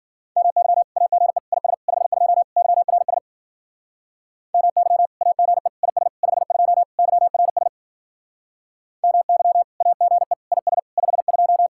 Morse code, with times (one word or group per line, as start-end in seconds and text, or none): 0.4-0.8 MY
1.0-1.4 AGE
1.5-1.7 IS
1.9-2.4 52
2.6-3.2 YRS
4.5-5.1 MY
5.2-5.7 AGE
5.8-6.1 IS
6.2-6.8 52
7.0-7.7 YRS
9.0-9.6 MY
9.8-10.3 AGE
10.5-10.8 IS
11.0-11.7 52